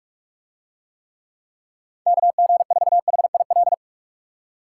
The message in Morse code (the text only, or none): KG4HIL